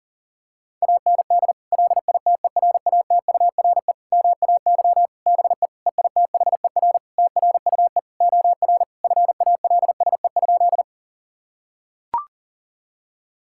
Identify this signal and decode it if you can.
Morse code, heard as a keyed tone, about 40 wpm, 700 Hz. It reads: AND LITERATURE MAY BE EITHER TRUE OR FALSE?
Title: Sentences from Top 1600 Words 40wpm